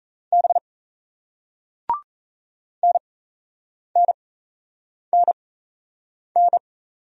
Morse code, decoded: B NNNN

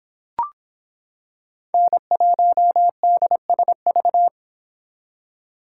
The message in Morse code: N1DSV

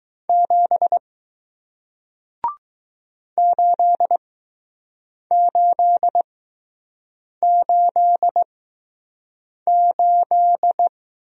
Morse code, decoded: 7 8888

